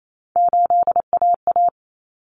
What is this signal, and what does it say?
Morse code, heard as a keyed tone, about 28 wpm, 700 Hz. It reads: 8AA
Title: Common Cut Numbers - 3-Digit 28wpm